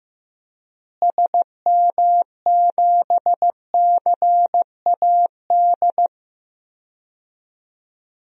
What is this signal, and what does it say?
Morse code, heard as a keyed tone, about 15 wpm, 700 Hz. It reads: SM7CAD
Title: DXCC-100 Call Signs - 6 Characters - Repeated 2x - 15wpm